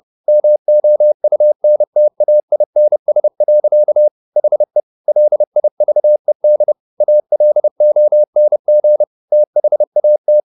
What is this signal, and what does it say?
Morse code, heard as a keyed tone, about 30 wpm, 600 Hz. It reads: MOUNTAINS. HE LIVED ALONG THAT
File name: the old man and the sea / the old man and the sea 1 30wpm